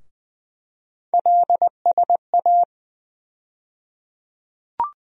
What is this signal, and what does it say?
Morse code, heard as a keyed tone, about 20 wpm, 700 Hz. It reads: LSA
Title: USA Call Sign Postfixes - 3 Letters 20wpm